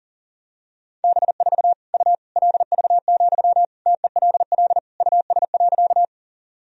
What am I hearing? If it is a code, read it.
Morse code, heard as a keyed tone, about 37 wpm, 700 Hz. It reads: B4 U LV, TELL US.
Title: Sentences from Top 400 Words and Abbreviations 40wpm